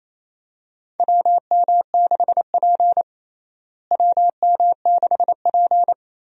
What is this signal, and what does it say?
Morse code, heard as a keyed tone, about 28 wpm, 700 Hz. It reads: WM6P WM6P
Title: USA Call Signs - 4 Characters - Repeated 3x - 28wpm